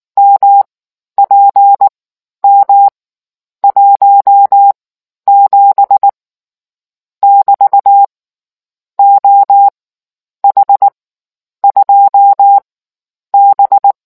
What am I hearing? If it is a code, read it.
Morse code, heard as a keyed tone, about 19 wpm, spaced as for 10 wpm, 800 Hz. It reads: MPM 17 <BT> OH2B